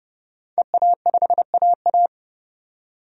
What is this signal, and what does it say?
Morse code, heard as a keyed tone, about 32 wpm, 700 Hz. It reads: EA5AA
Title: DXCC-100 Call Signs - 5 Characters - 30wpm